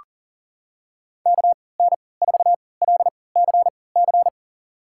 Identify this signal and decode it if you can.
Morse code, heard as a keyed tone, about 40 wpm, spaced as for 23 wpm, 700 Hz. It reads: KN4LCC
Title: USA Call Signs - 3x Character Spacing - 40wpm